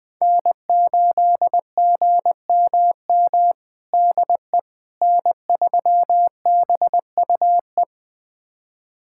Morse code, heard as a keyed tone, about 20 wpm, 700 Hz. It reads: N8GMM DE N3BUE
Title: USA Callsign DE Callsign - 5 Characters - 20wpm